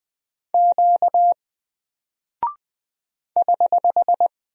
Morse code, heard as a keyed tone, about 20 wpm, 700 Hz. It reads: Q <HH>
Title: Single Character (Letter:Number:Punctuation:Prosign) Alt 20wpm